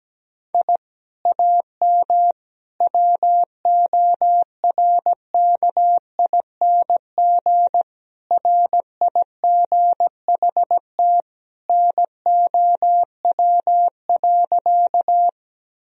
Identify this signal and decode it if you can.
Morse code, heard as a keyed tone, about 17 wpm, 700 Hz. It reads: I AM WORKING RIGHT NOW.
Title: Sentences from Top 1000 Words - Review Part 4 - 17wpm